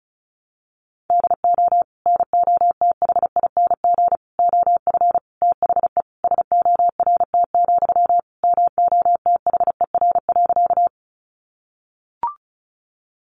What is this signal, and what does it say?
Morse code, heard as a keyed tone, about 34 wpm, 700 Hz. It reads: DO NOTHING OF THE SORT, MOTHER.
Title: Sentences from Top 1300 Words 35wpm